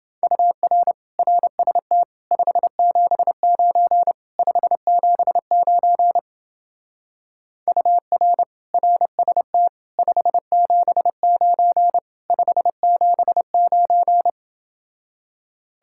Morse code, heard as a keyed tone, about 28 wpm, 700 Hz. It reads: UR RST 579 579 UR RST 579 579